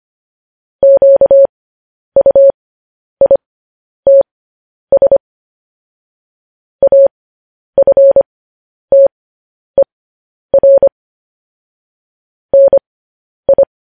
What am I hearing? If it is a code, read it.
Morse code, heard as a keyed tone, about 25 wpm, spaced as for 10 wpm, 550 Hz. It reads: QUITS AFTER NI
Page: news-headlines-in-morse-code-at-10-wpm